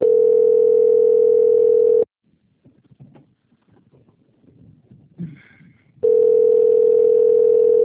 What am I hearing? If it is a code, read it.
Ringback tone (North America), 440 + 480 Hz.
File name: out- - - - - .40595